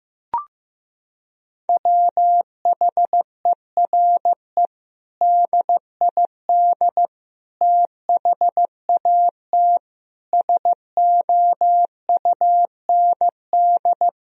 Morse code, heard as a keyed tone, about 15 wpm, 700 Hz. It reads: WHERE DID THAT SOUND
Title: Sentences from Top 1000 Words - Review Part 1 - 15wpm